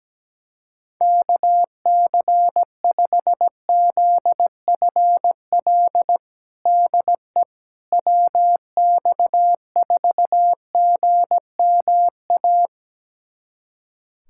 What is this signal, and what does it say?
Morse code, heard as a keyed tone, about 17 wpm, 700 Hz. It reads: KC5ZFL DE WX4GMA
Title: USA Callsign DE Callsign - 6 Characters - 17wpm